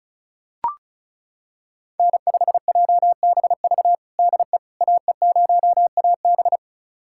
Morse code, heard as a keyed tone, about 36 wpm, 700 Hz. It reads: N5JBV DE AE0AB
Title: USA Callsign DE Callsign - 5 Characters - 35wpm